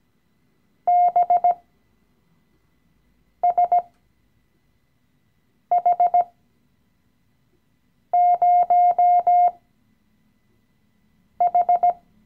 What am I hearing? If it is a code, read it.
Morse code, heard as a keyed tone, about 17 wpm, spaced as for 4 wpm, 710 Hz. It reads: BSH0H
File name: CW-Online-Kurs / Anfänger / Lektion 016